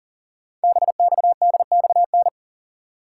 Morse code, heard as a keyed tone, about 40 wpm, 690 Hz. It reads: BXDXN
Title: Code Groups - 5 Letter:Numbers - 40wpm